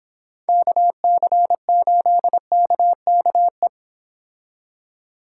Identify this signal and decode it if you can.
Morse code, heard as a keyed tone, about 26 wpm, 700 Hz. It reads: KC8KKE